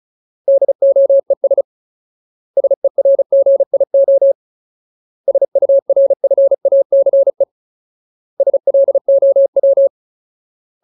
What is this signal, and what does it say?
Morse code, heard as a keyed tone, about 34 wpm, 550 Hz. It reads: DOES SERGIO SURFACE SLOW
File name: Ultimate Word List / 35WPM Ultimate